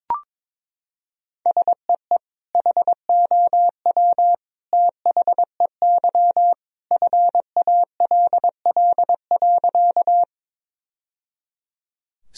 Morse code, heard as a keyed tone, about 22 wpm, 700 Hz. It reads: SEE HOW THEY FALL.
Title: Sentences from Top 1000 Words - Review Part 2 - 22wpm